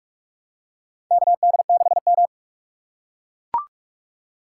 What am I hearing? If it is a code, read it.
Morse code, heard as a keyed tone, about 44 wpm, 700 Hz. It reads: KD6M
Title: USA Call Signs - 4 Characters - 2x Character Spacing - 45wpm